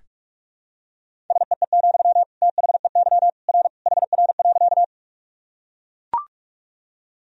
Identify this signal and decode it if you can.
Morse code, heard as a keyed tone, about 44 wpm, 700 Hz. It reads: SEE, THEY R HR.
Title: Sentences from Top 200 Words and Abbreviations 45wpm